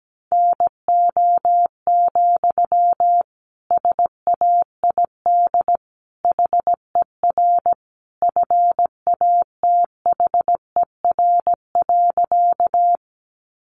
Morse code, heard as a keyed tone, about 17 wpm, 700 Hz. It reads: NO, SAID HER FATHER.
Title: Sentences from Top 300 Words 17wpm